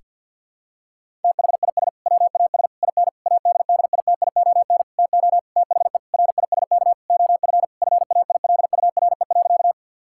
Morse code, heard as a keyed tone, about 46 wpm, 700 Hz. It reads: THIS WAS IN ADDITION TO THE RISK OF FAILURE.
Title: Sentences from Top 1000 Words - Review Part 6 - 50wpm